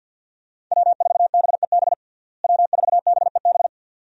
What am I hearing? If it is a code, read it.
Morse code, heard as a keyed tone, about 51 wpm, 700 Hz. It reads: W4BEB W4BEB
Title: USA Call Signs - 5 Characters - Repeated 2x - 50wpm